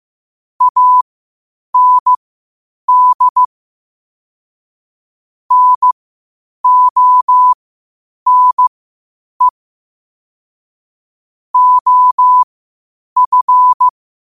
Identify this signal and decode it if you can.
Morse code, heard as a keyed tone, about 15 wpm, spaced as for 8 wpm, 1000 Hz. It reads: AND NONE OF
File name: qotdfiles / 8wpm